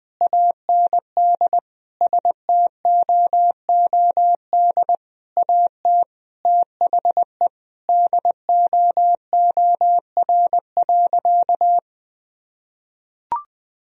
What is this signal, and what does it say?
Morse code, heard as a keyed tone, about 21 wpm, 700 Hz. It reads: AND STOOD AT THE DOOR.
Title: Sentences from Top 1000 Words - Review Part 2 - 20wpm